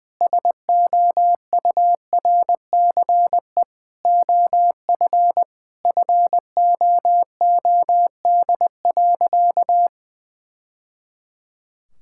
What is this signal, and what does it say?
Morse code, heard as a keyed tone, about 21 wpm, 700 Hz. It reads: SOURCE OF FOOD.
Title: Sentences from Top 1000 Words - Review Part 5 - 20wpm